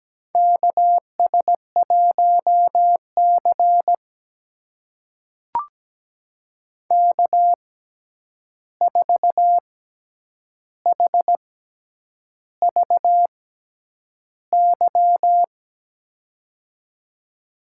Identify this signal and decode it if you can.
Morse code, heard as a keyed tone, about 17 wpm, spaced as for 8 wpm, 700 Hz. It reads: KS1C K4HVY
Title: USA Call Signs - 6x Character Spacing - 17wpm